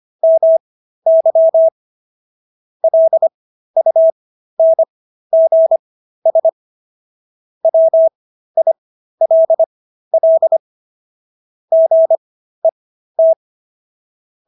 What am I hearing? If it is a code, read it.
Morse code, heard as a keyed tone, about 25 wpm, spaced as for 13 wpm, 650 Hz. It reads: MY LUNGS WILL GET